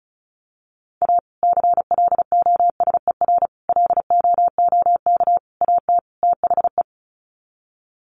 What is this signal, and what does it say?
Morse code, heard as a keyed tone, about 34 wpm, 700 Hz. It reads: A CLOSER LOOK AT THE
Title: Sets of 5 Words Encore 35wpm